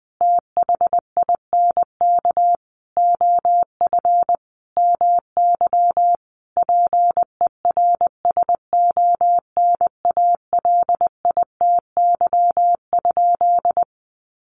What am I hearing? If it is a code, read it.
Morse code, heard as a keyed tone, about 20 wpm, 700 Hz. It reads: THINK OF MY PERSONALITY?
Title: Sentences from Top 1400 Words 20wpm